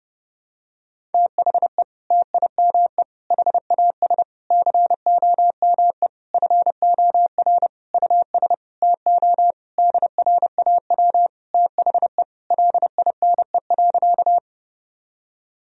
Morse code, heard as a keyed tone, about 32 wpm, 700 Hz. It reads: THE TIME HAS COME FOR US TO DRAW THE LINE.
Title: Sentences from Top 1000 Words - Review Part 1 - 30wpm